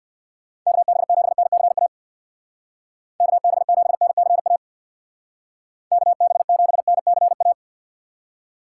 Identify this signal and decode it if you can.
Morse code, heard as a keyed tone, about 60 wpm, 700 Hz. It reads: KB7NCA KB7NCA KB7NCA
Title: Speed Racing - USA Call Signs 50wpm